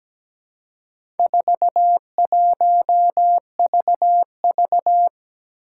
Morse code, heard as a keyed tone, about 17 wpm, 700 Hz. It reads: 41VV
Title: Four Challenging Letter-Number Combinations 17wpm